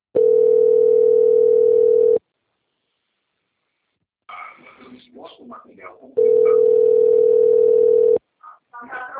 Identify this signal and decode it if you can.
Ringback tone (North America), 440 + 480 Hz, with speech.
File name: out- - - - - .16562